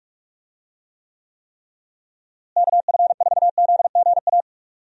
Morse code, heard as a keyed tone, about 46 wpm, 700 Hz. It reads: KF4ZGA
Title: USA Call Signs - 6 Characters - Repeated 3x - 45wpm